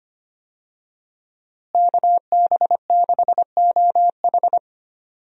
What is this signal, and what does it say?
Morse code, heard as a keyed tone, about 25 wpm, 700 Hz. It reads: KB6OH
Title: USA Call Signs - 5 Characters - Repeated 2x - 25wpm